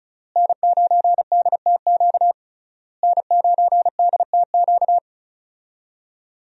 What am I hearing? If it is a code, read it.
Morse code, heard as a keyed tone, about 34 wpm, 700 Hz. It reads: N9DTQ N9DTQ
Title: USA Call Signs - 5 Characters - Repeated 3x - 35wpm